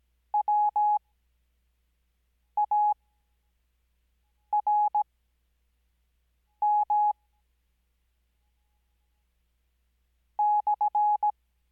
Morse code, heard as a keyed tone, about 17 wpm, spaced as for 5 wpm, 840 Hz. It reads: WARM /
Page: Farnsworth Spacing: 15WPM Character speed Spaced at 5WPM.